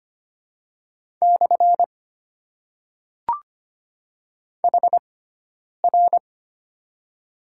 Morse code, heard as a keyed tone, about 25 wpm, spaced as for 9 wpm, 700 Hz. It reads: / HR
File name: Top 90 QSO Elements - 6x Character Spacing - 25wpm